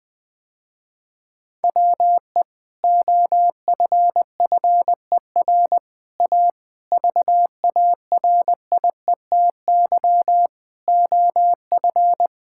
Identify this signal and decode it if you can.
Morse code, heard as a keyed tone, about 20 wpm, 700 Hz. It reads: WE OFFER A VARIETY OF
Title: Sentences with 3-Word Phrases - Spoken Beginning - 20wpm